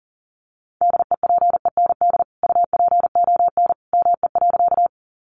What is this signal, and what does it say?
Morse code, heard as a keyed tone, about 40 wpm, 700 Hz. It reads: DEPEND UPON ME.
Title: Sentences from Top 1300 Words 40wpm